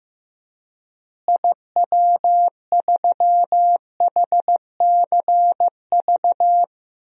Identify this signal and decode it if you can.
Morse code, heard as a keyed tone, about 15 wpm, 700 Hz. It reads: IW3HCV
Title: DXCC-100 Call Signs - 6 Characters - Repeated 3x - 15wpm